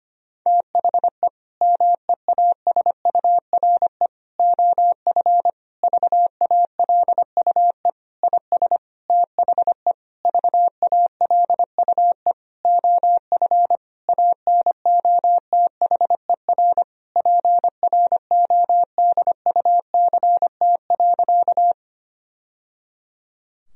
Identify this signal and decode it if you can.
Morse code, heard as a keyed tone, about 25 wpm, 700 Hz. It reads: THE MEASURE OF VALUE IS THE VALUE OF ANOTHER PRODUCT.
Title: Sentences from Top 1000 Words - Review Part 3 - 25wpm